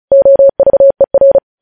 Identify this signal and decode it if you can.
Morse code, heard as a keyed tone, about 36 wpm, 550 Hz. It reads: OVER
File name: podcast / News headlines-1x-35x35